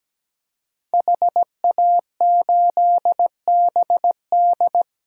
Morse code, heard as a keyed tone, about 17 wpm, 700 Hz. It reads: HA8BD
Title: DXCC-100 Call Signs - 5 Characters - Repeated 2x - 17wpm